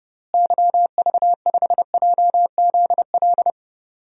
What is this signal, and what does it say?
Morse code, heard as a keyed tone, about 30 wpm, 700 Hz. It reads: YV5JZL
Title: DXCC-100 Call Signs - 6 Characters - Repeated 2x - 30wpm